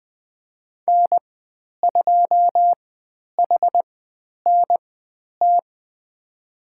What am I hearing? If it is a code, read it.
Morse code, heard as a keyed tone, about 20 wpm, spaced as for 10 wpm, 700 Hz. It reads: N2HNT
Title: 102 Lesson 17g - Letter u - Call Signs 20wpm